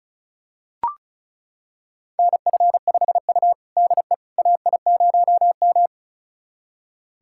Morse code, heard as a keyed tone, about 36 wpm, 700 Hz. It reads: NF5U DE AI0M
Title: USA Callsign DE Callsign - 4 Characters - 35wpm